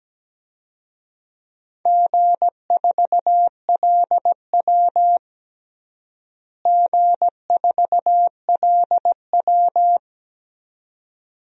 Morse code, heard as a keyed tone, about 17 wpm, 700 Hz. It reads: G4LW G4LW
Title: DXCC-100 Call Signs - 4 Characters - Repeated 2x - 17wpm